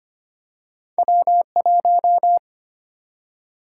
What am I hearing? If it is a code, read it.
Morse code, heard as a keyed tone, about 25 wpm, 700 Hz. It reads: W1
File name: Two Letter-Number Combinations - 25wpm